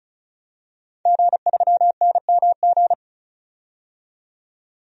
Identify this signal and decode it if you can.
Morse code, heard as a keyed tone, about 35 wpm, 700 Hz. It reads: G3NMG
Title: DXCC-100 Call Signs - 5 Characters - 35wpm